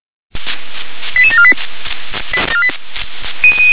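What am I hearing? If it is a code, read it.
Five-tone selective call (ZVEI), heard as quick runs of five tones, tones of about 70 ms. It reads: 88748
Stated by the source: ZVEI five-tone selective call 88730 received by SDR.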